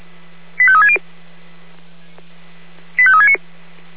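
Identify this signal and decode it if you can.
Five-tone selective call (ZVEI), heard as quick runs of five tones, tones of about 70 ms. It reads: 85360 85369
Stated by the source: ZVEI five-tone selective call 85369 received by SDR, recorded by Feuerwehr Orlamünde.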